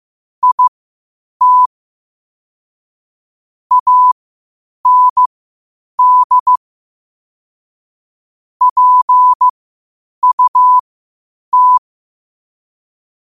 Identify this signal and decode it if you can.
Morse code, heard as a keyed tone, about 15 wpm, spaced as for 8 wpm, 1000 Hz. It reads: IT AND PUT
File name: qotdfiles / 8wpm